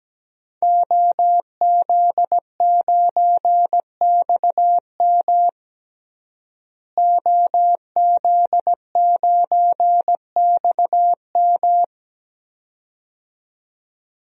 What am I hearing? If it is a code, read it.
Morse code, heard as a keyed tone, about 17 wpm, 700 Hz. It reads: OZ9XM OZ9XM
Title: DXCC-100 Call Signs - 5 Characters - Repeated 2x - 17wpm